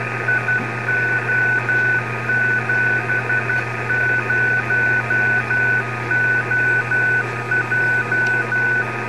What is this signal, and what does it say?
Morse code, heard as a keyed tone, about 12 wpm, 1500 Hz, with noise. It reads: IO70OJ
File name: beacons / GB3MCB